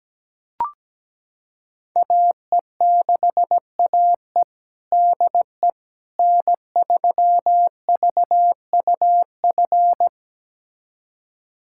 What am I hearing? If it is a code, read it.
Morse code, heard as a keyed tone, about 17 wpm, 700 Hz. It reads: AE6AE DE N3VUF
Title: USA Callsign DE Callsign - 5 Characters - 17wpm